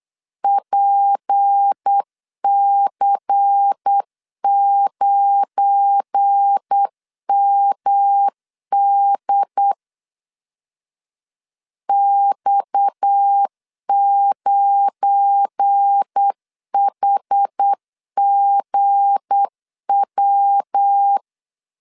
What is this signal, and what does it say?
Morse code, heard as a keyed tone, about 8 wpm, 800 Hz. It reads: PC9MD X9HGW
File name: LEZIONI / 7 GF9Z / 7 GF9 8wpm 3545